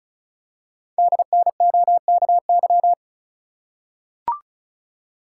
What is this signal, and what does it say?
Morse code, heard as a keyed tone, about 36 wpm, 690 Hz. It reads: DNOKY
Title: Code Groups - 5 Letters - 35wpm